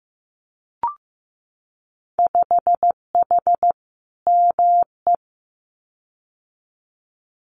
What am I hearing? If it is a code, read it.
Morse code, heard as a keyed tone, about 15 wpm, 700 Hz. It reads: 5H ME
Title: Winter Field Day Exchanges Encore 15wpm